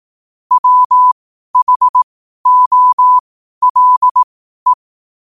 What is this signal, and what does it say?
Morse code, heard as a keyed tone, about 18 wpm, spaced as for 12 wpm, 1000 Hz. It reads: WHOLE
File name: qotdfiles / 12wpm